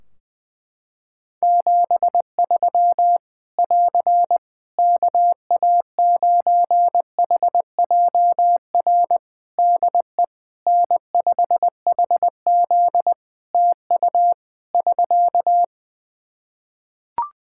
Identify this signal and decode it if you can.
Morse code, heard as a keyed tone, about 20 wpm, 700 Hz. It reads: 73 <AR> KA9HJR DE N5HZ TU <SK>
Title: Practice QSO - Exchange 4 - Ending - 20wpm